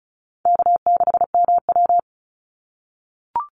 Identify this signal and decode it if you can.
Morse code, heard as a keyed tone, about 34 wpm, 700 Hz. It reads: K6MW
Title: NA SOTA Chaser Callsigns 35wpm